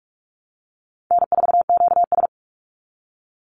Morse code, heard as a keyed tone, about 44 wpm, 700 Hz. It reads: N4QS
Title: Easy Call Signs Repeated 2x - 45wpm